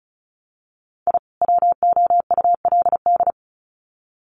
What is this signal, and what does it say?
Morse code, heard as a keyed tone, about 34 wpm, 700 Hz. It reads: I WOULD
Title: Sets of 2 Words 35wpm